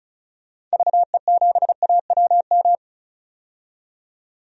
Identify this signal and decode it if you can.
Morse code, heard as a keyed tone, about 34 wpm, 700 Hz. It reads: VE7AWM
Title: DXCC-100 Call Signs - 6 Characters - 35wpm